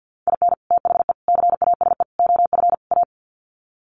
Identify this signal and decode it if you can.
Morse code, heard as a keyed tone, about 48 wpm, 700 Hz. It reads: IN THE CASE OF A
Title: Sets of 5 Words Encore 50wpm